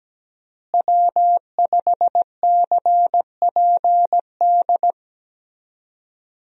Morse code, heard as a keyed tone, about 17 wpm, 700 Hz. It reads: W5CPD
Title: USA Call Signs - 5 Characters - Repeated 3x - 17wpm